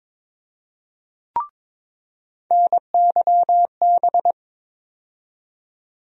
Morse code, heard as a keyed tone, about 21 wpm, 700 Hz. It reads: NYB
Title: USA Call Sign Postfixes - 3 Letters 22wpm